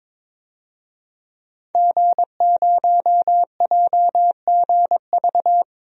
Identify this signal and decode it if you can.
Morse code, heard as a keyed tone, about 22 wpm, 700 Hz. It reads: G0JGV
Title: DXCC-100 Call Signs - 5 Characters - Repeated 3x - 22wpm